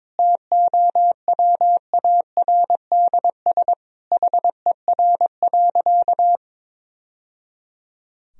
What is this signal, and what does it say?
Morse code, heard as a keyed tone, about 22 wpm, 700 Hz. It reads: TOWARDS HER.
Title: Sentences from Top 1000 Words - Review Part 2 - 22wpm